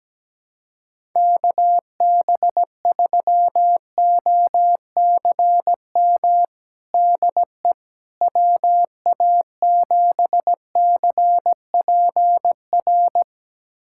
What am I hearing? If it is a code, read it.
Morse code, heard as a keyed tone, about 17 wpm, 700 Hz. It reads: KB3OCM DE WA7CPR
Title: USA Callsign DE Callsign 17wpm